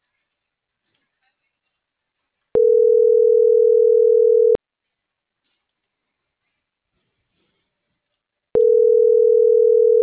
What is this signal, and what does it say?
Ringback tone (North America), 440 + 480 Hz.